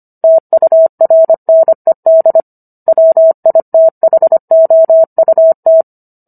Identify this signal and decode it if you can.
Morse code, heard as a keyed tone, about 25 wpm, 650 Hz. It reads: TURNED WITHOUT